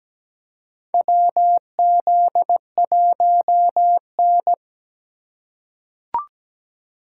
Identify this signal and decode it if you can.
Morse code, heard as a keyed tone, about 17 wpm, 700 Hz. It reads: WZ1N
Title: USA Call Signs - 4 Characters - 3x Character Spacing - 17wpm